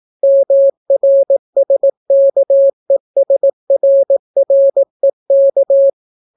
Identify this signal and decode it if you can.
Morse code, heard as a keyed tone, about 18 wpm, 550 Hz. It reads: MRSKESRREK